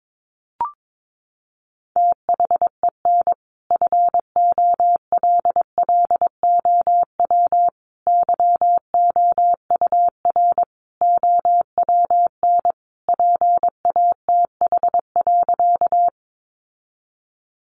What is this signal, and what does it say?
Morse code, heard as a keyed tone, about 22 wpm, 700 Hz. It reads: THEN FOLLOW YOUR OWN PATH.
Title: Sentences from Top 1400 Words 22wpm